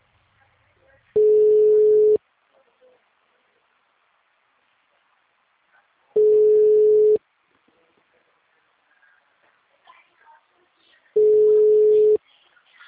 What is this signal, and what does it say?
Ringback tone (Europe, 425 Hz), with speech.
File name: out- -2013- - - .75